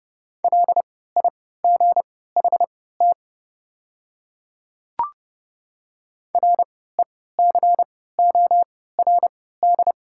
Morse code, heard as a keyed tone, about 30 wpm, spaced as for 17 wpm, 690 Hz. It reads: LIGHT RECORD
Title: 3x Character Spacing - Top 500 Words - 30wpm